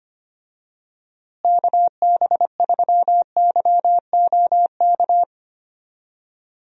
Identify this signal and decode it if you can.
Morse code, heard as a keyed tone, about 25 wpm, 700 Hz. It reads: KB3YOK
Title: USA Call Signs - 6 Characters - Repeated 3x - 25wpm